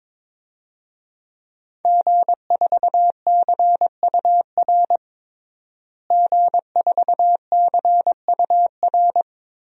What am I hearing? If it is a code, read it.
Morse code, heard as a keyed tone, about 22 wpm, 700 Hz. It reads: G4CUR G4CUR
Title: DXCC-100 Call Signs - 5 Characters - Repeated 2x - 22wpm